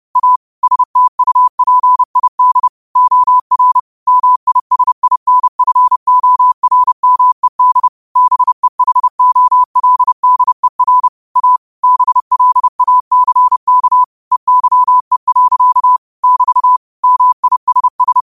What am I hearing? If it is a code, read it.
Morse code, heard as a keyed tone, about 30 wpm, 1000 Hz. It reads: A STUPID OR MISINFORMED BEHOLDER A BLACK EYE. <BT> MISS
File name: qotdfiles / 30wpm